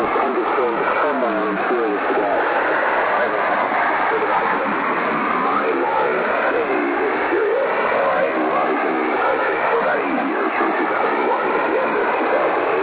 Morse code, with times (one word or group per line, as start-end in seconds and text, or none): none